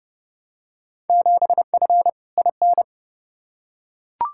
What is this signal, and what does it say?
Morse code, heard as a keyed tone, about 30 wpm, 700 Hz. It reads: 7F IN
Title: Field Day Exchanges Encore - 30wpm